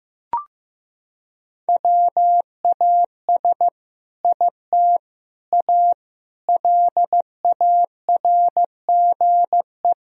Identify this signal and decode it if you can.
Morse code, heard as a keyed tone, about 15 wpm, 700 Hz. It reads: WAS IT A LARGE
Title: Sentences from Top 1000 Words - Review Part 4 - 15wpm